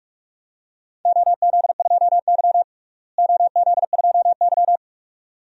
Morse code, heard as a keyed tone, about 46 wpm, 700 Hz. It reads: OZ2Y OZ2Y
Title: DXCC-100 Call Signs - 4 Characters - Repeated 2x - 45wpm